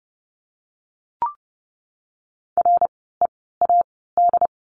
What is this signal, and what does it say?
Morse code, heard as a keyed tone, about 30 wpm, spaced as for 17 wpm, 700 Hz. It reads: READ
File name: Lesson 16f - Letter c - Words 30wpm